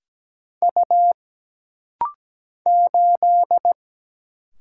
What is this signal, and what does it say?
Morse code, heard as a keyed tone, about 17 wpm, 700 Hz. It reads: U 8
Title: Single Letter-Number - Rapid-Fire 17wpm